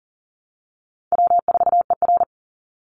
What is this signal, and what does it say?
Morse code, heard as a keyed tone, about 40 wpm, 700 Hz. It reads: W4ER
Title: Call Signs Repeated 3x - 40wpm